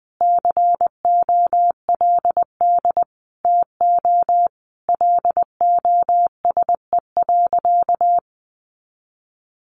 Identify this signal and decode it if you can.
Morse code, heard as a keyed tone, about 20 wpm, 700 Hz. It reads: COLD TO LOSE.
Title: Sentences from Top 1100 Words 20wpm